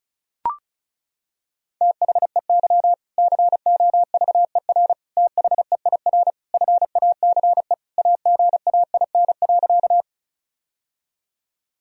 Morse code, heard as a keyed tone, about 36 wpm, 700 Hz. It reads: THEY COVER THEIR FACE AGAIN.
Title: Sentences from Top 1000 Words - Review Part 1 - 35wpm